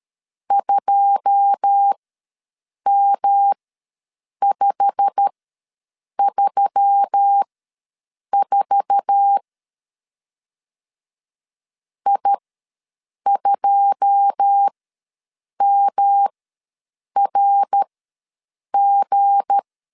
Morse code, heard as a keyed tone, about 13 wpm, spaced as for 7 wpm, 800 Hz. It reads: 2M534 I2MRG
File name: LEZIONI / 9 2 ws 12wpm 3406